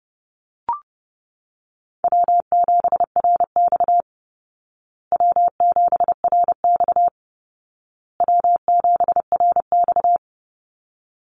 Morse code, heard as a keyed tone, about 30 wpm, 700 Hz. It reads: W7RX W7RX W7RX
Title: Call Signs Repeated 2x - 30wpm